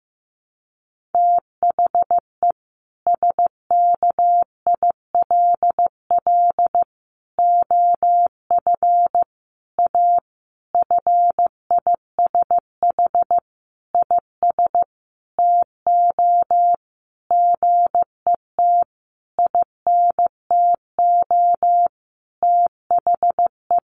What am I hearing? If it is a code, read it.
Morse code, heard as a keyed tone, about 15 wpm, 700 Hz. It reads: THE SKILL OF A FISH IS TO GET INTO THE
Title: Sentences from Top 1400 Words 15wpm